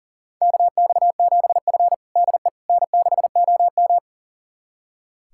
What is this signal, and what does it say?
Morse code, heard as a keyed tone, about 40 wpm, 700 Hz. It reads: KX7F DE N6OM
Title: USA Callsign DE Callsign - 4 Characters - 40wpm